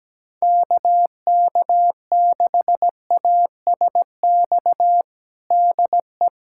Morse code, heard as a keyed tone, about 17 wpm, 700 Hz. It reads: KK6ASX DE
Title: USA Callsign DE Callsign 17wpm